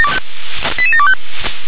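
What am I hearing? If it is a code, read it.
Five-tone selective call (ZVEI), heard as quick runs of five tones, tones of about 70 ms. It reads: 88725
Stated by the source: ZVEI five-tone selective call received on SDR.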